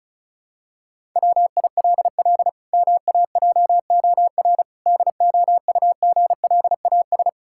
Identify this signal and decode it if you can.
Morse code, heard as a keyed tone, about 36 wpm, 700 Hz. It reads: WILL MAJOR DOUGLAS
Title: Pangram Challenge v2 - 35wpm